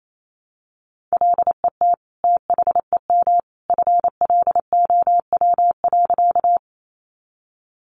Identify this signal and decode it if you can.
Morse code, heard as a keyed tone, about 28 wpm, 700 Hz. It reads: LET THEM FLOW.
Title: Sentences from Top 1200 Words 28wpm